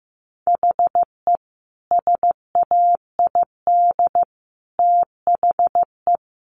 Morse code, heard as a keyed tone, about 15 wpm, 700 Hz. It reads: HE SAID THE
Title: Sentences from Top 1300 Words 15wpm